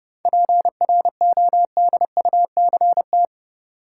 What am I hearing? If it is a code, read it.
Morse code, heard as a keyed tone, about 30 wpm, 700 Hz. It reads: PRODUCT
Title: Sentences from Top 1400 Words - New Word Drill Alt - 30wpm